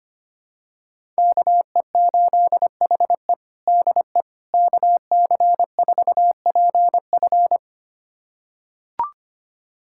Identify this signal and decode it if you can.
Morse code, heard as a keyed tone, about 25 wpm, 700 Hz. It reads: KE8HE DE KC4PF
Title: USA Callsign DE Callsign - 5 Characters - 25wpm